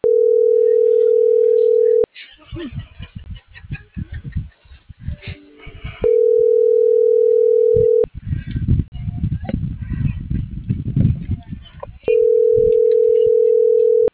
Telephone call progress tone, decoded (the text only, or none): ringback tone